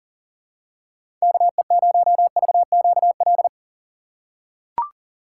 Morse code, heard as a keyed tone, about 40 wpm, 700 Hz. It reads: KE0VQL
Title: USA Call Signs - 3x Character Spacing - 40wpm